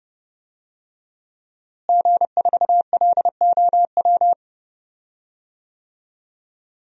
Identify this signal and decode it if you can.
Morse code, heard as a keyed tone, about 32 wpm, 700 Hz. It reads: G4LOW
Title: DXCC-100 Call Signs - 5 Characters - Repeated 2x - 30wpm